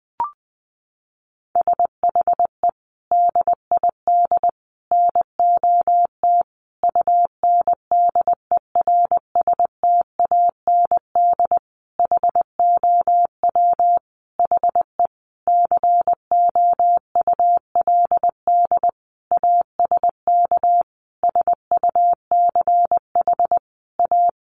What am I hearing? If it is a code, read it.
Morse code, heard as a keyed tone, about 20 wpm, 700 Hz. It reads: SHE DID NOT UNDERSTAND HOW HE COULD ASK SUCH A
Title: Sentences from Top 600 Words 20wpm